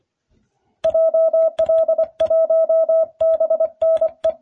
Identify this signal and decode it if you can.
Morse code, heard as a keyed tone, about 24 wpm, 650 Hz. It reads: JL1BNE